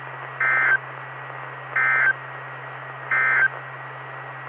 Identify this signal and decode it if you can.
Emergency Alert System (SAME) tones, heard as harsh buzzing bursts. It is heard from 0.3 to 3.6 s.
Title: Saline, AK SKYWARN Repeater System - W5RHS